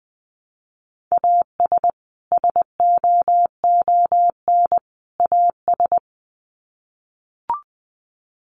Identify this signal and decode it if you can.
Morse code, heard as a keyed tone, about 20 wpm, 700 Hz. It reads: AS SOON AS
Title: Sets of 3 Words 20wpm